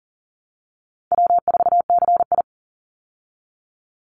Morse code, heard as a keyed tone, about 40 wpm, 700 Hz. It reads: W4CI
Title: Easy Call Signs Repeated 2x - 40wpm